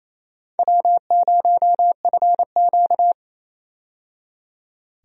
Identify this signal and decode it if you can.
Morse code, heard as a keyed tone, about 28 wpm, 700 Hz. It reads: W0FQ
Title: USA Call Signs - 4 Characters - 28wpm